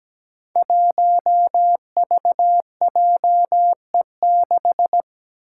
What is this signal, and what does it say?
Morse code, heard as a keyed tone, about 17 wpm, 700 Hz. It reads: 1VJE6
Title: Five Challenging Letter-Number Combinations 17wpm